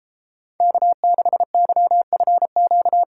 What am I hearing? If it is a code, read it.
Morse code, heard as a keyed tone, about 30 wpm, 700 Hz. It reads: K6YFQ